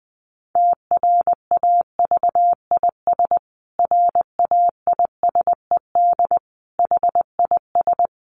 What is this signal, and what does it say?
Morse code, heard as a keyed tone, about 20 wpm, 700 Hz. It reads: TRAVIS RAISED HIS
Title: A Sound Of Thunder 20wpm